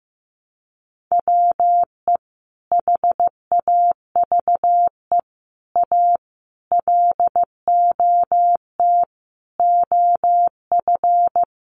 Morse code, heard as a keyed tone, about 15 wpm, 700 Hz. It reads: WE HAVE A LOT OF
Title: Sets of 5 Words 15wpm